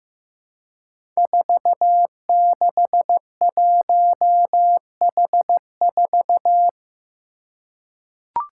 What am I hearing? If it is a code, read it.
Morse code, heard as a keyed tone, about 15 wpm, 700 Hz. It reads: 461H4
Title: Five Challenging Letter-Number Combinations 15wpm